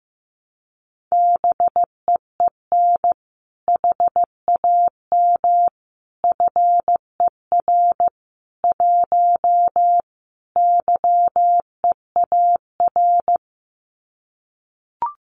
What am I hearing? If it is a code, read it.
Morse code, heard as a keyed tone, about 14 wpm, 700 Hz. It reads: BEEN HAM FER 1 YEAR
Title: Common QSO Phrases Encore 15wpm